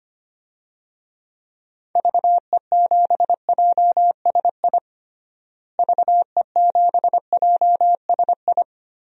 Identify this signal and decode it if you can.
Morse code, heard as a keyed tone, about 25 wpm, 700 Hz. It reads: VE7JSI VE7JSI
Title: DXCC-100 Call Signs - 6 Characters - Repeated 3x - 25wpm